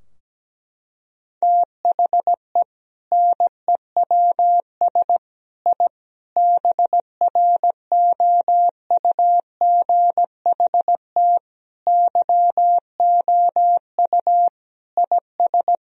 Morse code, heard as a keyed tone, about 17 wpm, 700 Hz. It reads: THE NEWS I BROUGHT YOU IS
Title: Sentences from Top 1000 Words - Review Part 3 - 17wpm